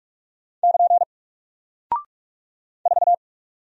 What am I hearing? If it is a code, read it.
Morse code, heard as a keyed tone, about 46 wpm, 700 Hz. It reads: <KN> 4
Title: Single Character (Letter:Number:Punctuation:Prosign) Alt 45wpm